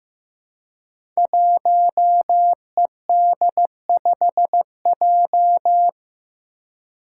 Morse code, heard as a keyed tone, about 15 wpm, 700 Hz. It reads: 1ED5J
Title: Five Challenging Letter-Number Combinations 15wpm